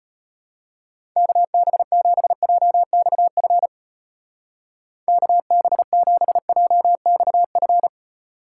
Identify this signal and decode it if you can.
Morse code, heard as a keyed tone, about 32 wpm, 700 Hz. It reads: KB7JXF KB7JXF